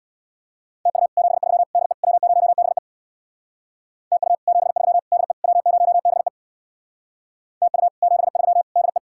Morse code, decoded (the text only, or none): TU 73 DE W0BE TU 73 DE W0BE TU 73 DE